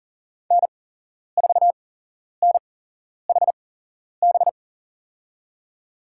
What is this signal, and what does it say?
Morse code, heard as a keyed tone, about 40 wpm, spaced as for 11 wpm, 700 Hz. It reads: N4NHB
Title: USA Call Signs - 5 Characters - 8x Character Spacing - 40wpm